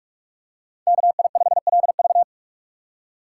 Morse code, heard as a keyed tone, about 45 wpm, 700 Hz. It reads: KI5LV